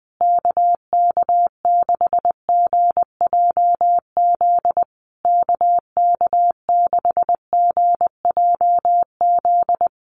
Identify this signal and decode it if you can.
Morse code, heard as a keyed tone, about 20 wpm, 700 Hz. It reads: KK6GJZ KK6GJZ
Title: USA Call Signs Repeated 3x 20wpm